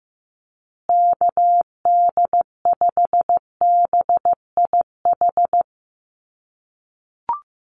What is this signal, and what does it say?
Morse code, heard as a keyed tone, about 15 wpm, 700 Hz. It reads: KD5BIH
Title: USA Call Signs Repeated 3x 15wpm